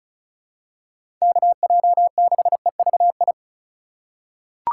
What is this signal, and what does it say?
Morse code, heard as a keyed tone, about 36 wpm, 700 Hz. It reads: KJ6EVI
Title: USA Call Signs - 2x Character Spacing - 35wpm